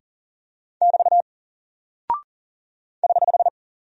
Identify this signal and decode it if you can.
Morse code, heard as a keyed tone, about 40 wpm, 700 Hz. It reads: <BT> <HH>
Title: Single Character (Letter:Number:Punctuation:Prosign) 40wpm